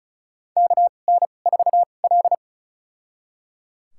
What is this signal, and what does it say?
Morse code, heard as a keyed tone, about 36 wpm, spaced as for 25 wpm, 700 Hz. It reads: KN4L
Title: USA Call Signs - 4 Characters - 2x Character Spacing - 35wpm